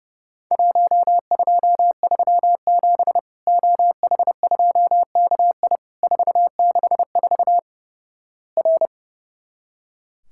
Morse code, heard as a keyed tone, about 30 wpm, 700 Hz. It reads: 1237 OH2KI 464 R
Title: Worked All Europe DX Contest - QTC Exchange - 30wpm